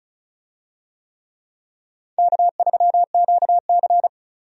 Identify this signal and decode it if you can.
Morse code, heard as a keyed tone, about 36 wpm, 700 Hz. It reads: K3QC
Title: USA Call Signs - 4 Characters - 35wpm